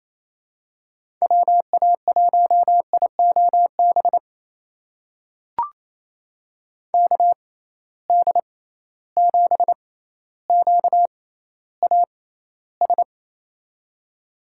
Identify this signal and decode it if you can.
Morse code, heard as a keyed tone, about 28 wpm, spaced as for 14 wpm, 700 Hz. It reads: WA1IOB KD7QAS
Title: USA Call Signs - 6 Characters - 6x Character Spacing - 28wpm